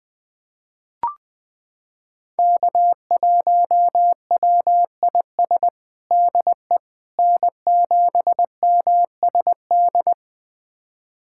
Morse code, heard as a keyed tone, about 20 wpm, 700 Hz. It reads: K1WIS DE N7MSD
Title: USA Callsign DE Callsign 20wpm